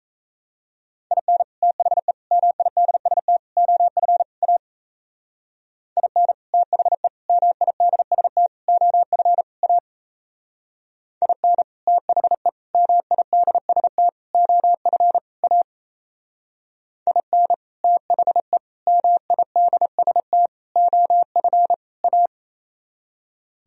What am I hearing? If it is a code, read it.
Morse code, heard as a keyed tone, about 35 wpm, 700 Hz. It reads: IN THE MIDST OF A IN THE MIDST OF A IN THE MIDST OF A IN THE MIDST OF A
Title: Speed-Racing - Sets of 5 Words 28wpm